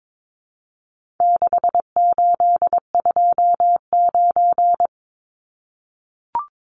Morse code, heard as a keyed tone, about 22 wpm, 700 Hz. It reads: 6829
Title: Numbers - 4-Digit 22wpm